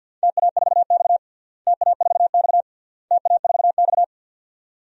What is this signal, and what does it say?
Morse code, heard as a keyed tone, about 48 wpm, 700 Hz. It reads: TA4X TA4X TA4X
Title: DXCC-100 Call Signs - 4 Characters - Repeated 3x - 50wpm